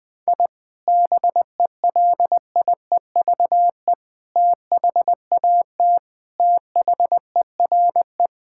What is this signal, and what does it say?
Morse code, heard as a keyed tone, about 20 wpm, 700 Hz. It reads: I BELIEVE THAT THERE
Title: Sentences with 4-Word Phrases - 20wpm